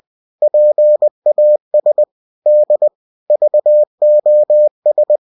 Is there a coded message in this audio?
Yes: Morse code from 0.4 to 5.2 s, heard as a keyed tone, about 20 wpm, 600 Hz. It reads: PAS D VOS